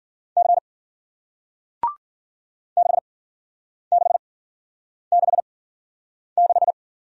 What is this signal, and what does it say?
Morse code, heard as a keyed tone, about 54 wpm, spaced as for 9 wpm, 700 Hz. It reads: C 6666